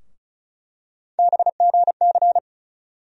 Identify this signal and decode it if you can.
Morse code, heard as a keyed tone, about 36 wpm, 700 Hz. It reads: BGC
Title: USA Call Sign Postfixes - 3 Letters 35wpm